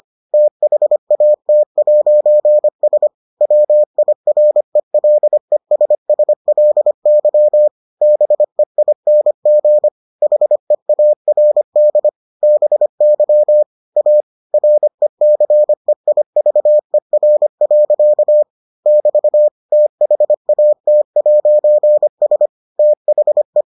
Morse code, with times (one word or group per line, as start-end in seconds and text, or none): 0.3-3.1 THAT'S
3.4-7.7 WIRELESSLY
8.0-9.9 BEING
10.2-12.1 HEARD
12.4-13.6 BY
14.0-14.2 A
14.5-18.4 RECEIVER.
18.9-19.5 <BT>
19.7-22.5 THAT'S
22.8-23.6 THE